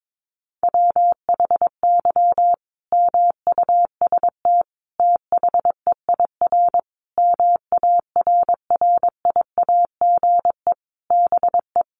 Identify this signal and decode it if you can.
Morse code, heard as a keyed tone, about 22 wpm, 700 Hz. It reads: WHY MUST THEIR MARRIAGE BE
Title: Sentences from Top 1000 Words 22wpm